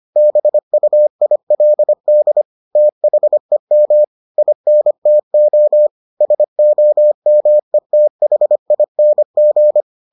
Morse code, heard as a keyed tone, about 25 wpm, 600 Hz. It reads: BUILD THEM INTO SOMETHING